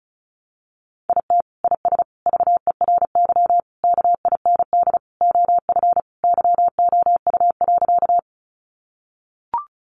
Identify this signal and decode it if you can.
Morse code, heard as a keyed tone, about 34 wpm, 700 Hz. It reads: IT IS VERY KIND OF YOU.
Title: Sentences from Top 200 Words 35wpm